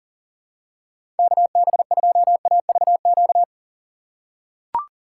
Morse code, heard as a keyed tone, about 40 wpm, 700 Hz. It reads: KB2AVQ
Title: USA Call Signs - 6 Characters - 4x Character Spacing - 40wpm